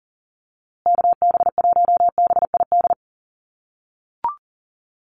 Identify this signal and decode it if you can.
Morse code, heard as a keyed tone, about 40 wpm, 700 Hz. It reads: KB1BID
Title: USA Call Signs Repeated 2x 40wpm